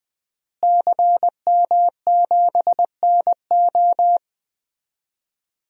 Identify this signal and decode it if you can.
Morse code, heard as a keyed tone, about 20 wpm, 700 Hz. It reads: CM7NO
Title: DXCC-100 Call Signs - 5 Characters - Repeated 3x - 20wpm